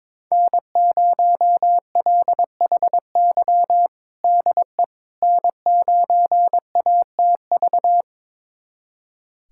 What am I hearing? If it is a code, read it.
Morse code, heard as a keyed tone, about 22 wpm, 700 Hz. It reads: N0LHY DE N9ATV
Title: USA Callsign DE Callsign - 5 Characters - 22wpm